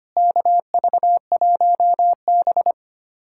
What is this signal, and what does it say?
Morse code, heard as a keyed tone, about 25 wpm, 700 Hz. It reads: KV1B
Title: USA Callsign DE Callsign 25wpm